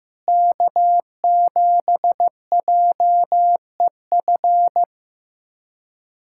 Morse code, heard as a keyed tone, about 15 wpm, 700 Hz. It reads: K7JEF
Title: USA Callsign DE Callsign - 5 Characters - 15wpm